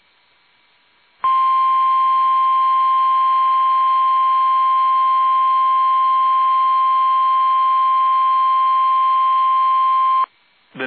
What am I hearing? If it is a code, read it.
Emergency Alert System (SAME) tones (the 1050 Hz attention tone). It is heard from 1.2 to 10.3 s.